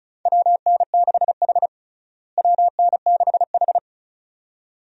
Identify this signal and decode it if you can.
Morse code, heard as a keyed tone, about 34 wpm, 700 Hz. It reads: WN6H WN6H
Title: USA Call Signs - 4 Characters - Repeated 2x - 35wpm